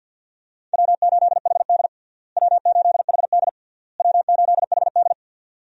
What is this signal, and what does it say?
Morse code, heard as a keyed tone, about 51 wpm, 700 Hz. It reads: W8HD W8HD W8HD
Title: USA Call Signs - 4 Characters - Repeated 3x - 50wpm